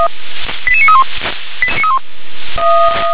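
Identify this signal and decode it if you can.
Five-tone selective call (ZVEI), heard as quick runs of five tones, tones of about 70 ms. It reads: 88931
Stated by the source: ZVEI five-tone selective call received on an SDR.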